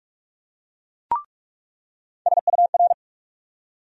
Morse code, heard as a keyed tone, about 43 wpm, 700 Hz. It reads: SUR
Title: USA Call Sign Postfixes - 3 Letters 45wpm